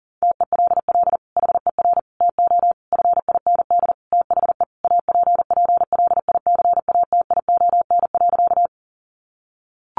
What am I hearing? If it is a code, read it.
Morse code, heard as a keyed tone, about 40 wpm, 700 Hz. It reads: TELL HER TO FIND THE APPLICATION.
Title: Sentences from Top 1200 Words 40wpm